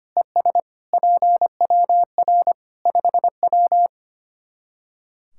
Morse code, heard as a keyed tone, about 25 wpm, 700 Hz. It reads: ES PWR 5W
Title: K1USN SST QSO Practice 25wpm